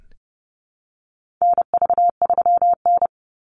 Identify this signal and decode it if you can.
Morse code, heard as a keyed tone, about 30 wpm, 700 Hz. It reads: NV3N
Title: Call Signs - 30wpm